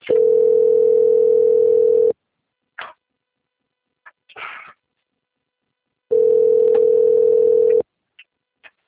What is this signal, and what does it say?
Ringback tone (North America), 440 + 480 Hz.
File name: out- - - - - .10686